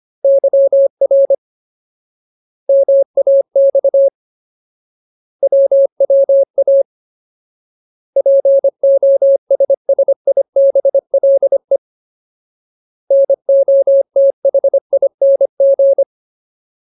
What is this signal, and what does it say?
Morse code, heard as a keyed tone, about 25 wpm, 550 Hz. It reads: YR MAX WWA POSSIBLE NOTHING
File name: Ultimate Word List / 25WPM Ultimate